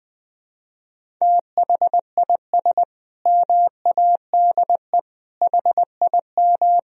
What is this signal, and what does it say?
Morse code, heard as a keyed tone, about 21 wpm, 700 Hz. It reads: THIS MADE HIM
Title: Sentences from Top 500 Words and Abbreviations 20wpm